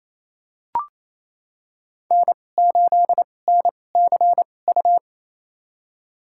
Morse code, heard as a keyed tone, about 28 wpm, spaced as for 20 wpm, 700 Hz. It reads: N8NCU
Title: USA Call Signs - 5 Characters - 2x Character Spacing - 28wpm